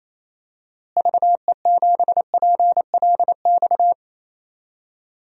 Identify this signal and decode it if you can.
Morse code, heard as a keyed tone, about 28 wpm, 700 Hz. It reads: VE7PLX
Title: DXCC-100 Call Signs - 6 Characters - Repeated 2x - 28wpm